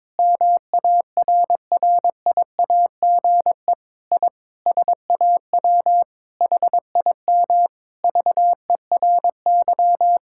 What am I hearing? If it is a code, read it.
Morse code, heard as a keyed tone, about 22 wpm, 700 Hz. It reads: MARRIAGE I SAW HIM VERY
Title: Sentences from Top 1000 Words - Review Part 5 - 22wpm